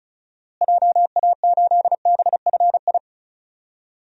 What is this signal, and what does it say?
Morse code, heard as a keyed tone, about 34 wpm, 700 Hz. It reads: JA8BFI
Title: DXCC-100 Call Signs - 6 Characters - Repeated 3x - 35wpm